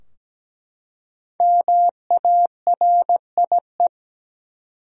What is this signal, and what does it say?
Morse code, heard as a keyed tone, about 17 wpm, 700 Hz. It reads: MARIE
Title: USA Common Names - Max 5 Letters - 17wpm